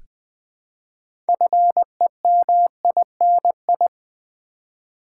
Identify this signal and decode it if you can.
Morse code, heard as a keyed tone, about 20 wpm, 700 Hz. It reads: FEMINI
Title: Build A Word - 3 Letters At A Time - 8-Letter Words - 20wpm